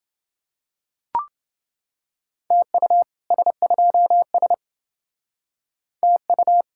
Morse code, heard as a keyed tone, about 30 wpm, 700 Hz. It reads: TU S2S TU